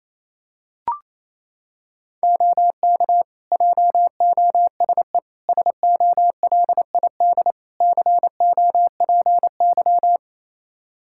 Morse code, heard as a keyed tone, about 28 wpm, 700 Hz. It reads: OK JOSE SOLID COPY
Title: Practice QSO - Exchange 3 - More Info - 28wpm